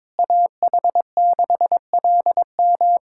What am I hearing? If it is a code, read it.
Morse code, heard as a keyed tone, about 22 wpm, 700 Hz. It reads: AH6LM
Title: USA Callsign DE Callsign - 5 Characters - 22wpm